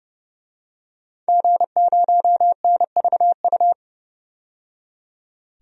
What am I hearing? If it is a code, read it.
Morse code, heard as a keyed tone, about 30 wpm, 700 Hz. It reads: G0NVU
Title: DXCC-100 Call Signs - 5 Characters - 30wpm